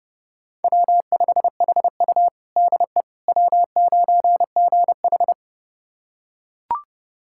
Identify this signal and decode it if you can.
Morse code, heard as a keyed tone, about 30 wpm, 700 Hz. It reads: W5HU DE W9GH
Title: USA Callsign DE Callsign - 4 Characters - 30wpm